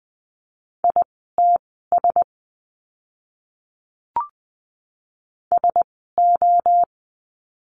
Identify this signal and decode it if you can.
Morse code, heard as a keyed tone, about 20 wpm, spaced as for 14 wpm, 700 Hz. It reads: ITS SO
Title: 230 Lesson 37f - Letter x - Words 20wpm